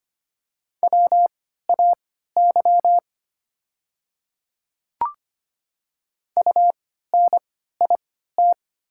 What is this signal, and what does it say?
Morse code, heard as a keyed tone, about 25 wpm, spaced as for 14 wpm, 690 Hz. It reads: WAY UNIT
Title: 3x Character Spacing - Top 500 Words - 25wpm